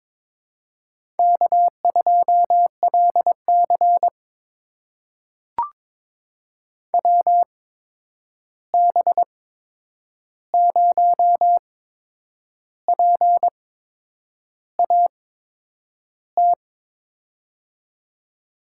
Morse code, heard as a keyed tone, about 22 wpm, spaced as for 8 wpm, 700 Hz. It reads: K2LC WB0PAT
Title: USA Call Signs - 8x Character Spacing - 22wpm